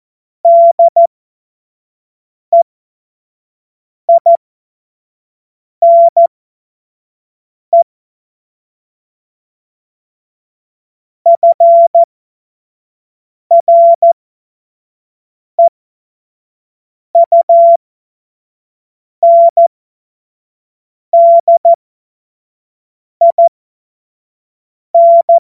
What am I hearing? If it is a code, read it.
Morse code, heard as a keyed tone, about 14 wpm, spaced as for 5 wpm, 680 Hz. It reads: DEINE FREUNDIN
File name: Morsekurs / Witze / Witz 27 Februar 2024 25 F70